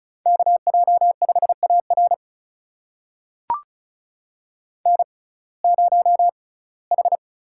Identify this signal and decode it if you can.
Morse code, heard as a keyed tone, about 36 wpm, spaced as for 21 wpm, 700 Hz. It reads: KJ5AR N0H
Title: USA Call Signs - 5 Characters - 6x Character Spacing - 35wpm